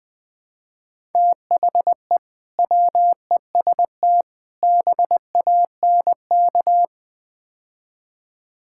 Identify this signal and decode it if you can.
Morse code, heard as a keyed tone, about 20 wpm, 700 Hz. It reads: THE WEST BANK
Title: Build A Phrase - 5-Word Phrases - 20wpm